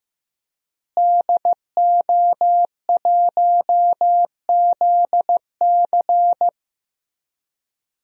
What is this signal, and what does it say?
Morse code, heard as a keyed tone, about 15 wpm, 700 Hz. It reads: DO1ZC
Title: DXCC-100 Call Signs - 5 Characters - Repeated 3x - 15wpm